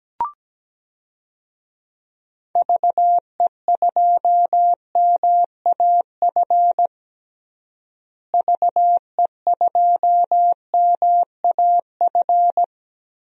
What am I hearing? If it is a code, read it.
Morse code, heard as a keyed tone, about 17 wpm, 700 Hz. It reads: VE2MAF VE2MAF
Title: DXCC-100 Call Signs - 6 Characters - Repeated 3x - 17wpm